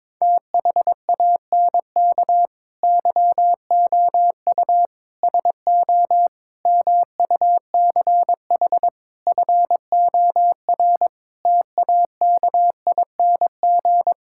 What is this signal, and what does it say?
Morse code, heard as a keyed tone, about 22 wpm, 700 Hz. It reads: THANK YOU SO MUCH FOR TAKING
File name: Sets of 9 Words - 22wpm